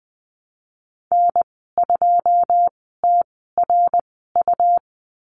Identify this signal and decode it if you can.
Morse code, heard as a keyed tone, about 20 wpm, spaced as for 15 wpm, 700 Hz. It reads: N2TRU
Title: 154 Lesson 25g - Letter p - Call Signs 20wpm